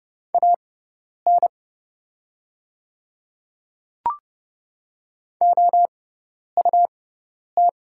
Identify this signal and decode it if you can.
Morse code, heard as a keyed tone, about 31 wpm, spaced as for 10 wpm, 700 Hz. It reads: AN OUT
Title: 6x Character Spacing - Top 100 Words - 30wpm